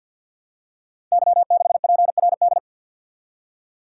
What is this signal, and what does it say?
Morse code, heard as a keyed tone, about 53 wpm, 700 Hz. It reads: Y6PRD
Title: Code Groups - 5 Letter:Numbers - 50wpm